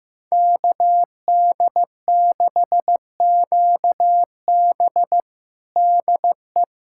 Morse code, heard as a keyed tone, about 15 wpm, 700 Hz. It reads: KD6QB DE
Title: USA Callsign DE Callsign - 5 Characters - 15wpm